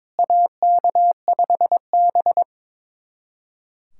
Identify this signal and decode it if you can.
Morse code, heard as a keyed tone, about 22 wpm, 700 Hz. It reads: AK5B
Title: USA Callsign DE Callsign - 4 Characters - 22wpm